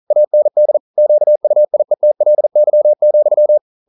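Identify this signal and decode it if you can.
Morse code, heard as a keyed tone, about 42 wpm, 600 Hz. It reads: AND QUIETLY,
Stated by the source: Morse code at 40 wpm, practice text, CW med Gnister.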